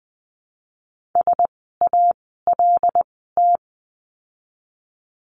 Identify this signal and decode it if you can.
Morse code, heard as a keyed tone, about 20 wpm, spaced as for 14 wpm, 700 Hz. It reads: SALT
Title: 075 Lesson 13f - Letter L - Words 20wpm